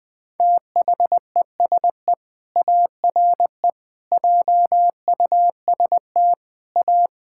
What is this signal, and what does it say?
Morse code, heard as a keyed tone, about 20 wpm, 700 Hz. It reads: THESE ARE JUST A
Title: Sentences with 7-Word Phrases - Spoken Beginning - 20wpm